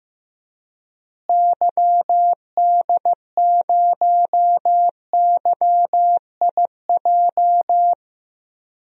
Morse code, heard as a keyed tone, about 15 wpm, 700 Hz. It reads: YD0YIJ
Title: DXCC-100 Call Signs - 6 Characters - Repeated 3x - 15wpm